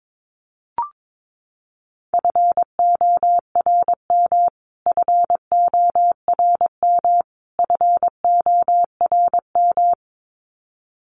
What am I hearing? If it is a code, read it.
Morse code, heard as a keyed tone, about 22 wpm, 700 Hz. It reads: FORM FORM FORM
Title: Top 500 Words Repeated 22wpm